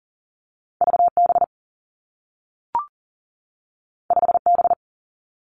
Morse code, heard as a keyed tone, about 38 wpm, 700 Hz. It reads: VB 5B
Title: Two Challenging Letter-Number Combinations 40wpm